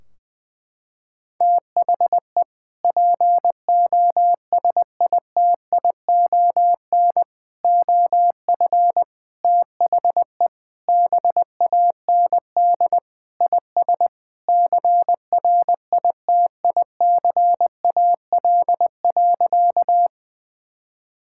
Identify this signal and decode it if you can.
Morse code, heard as a keyed tone, about 21 wpm, 700 Hz. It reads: THE POSITION OF THE BAND IS CRITICAL.
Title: Sentences from Top 1600 Words 20wpm